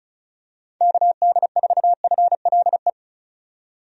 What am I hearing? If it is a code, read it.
Morse code, heard as a keyed tone, about 36 wpm, 700 Hz. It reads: KD4FLE
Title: USA Call Signs - 6 Characters - 35wpm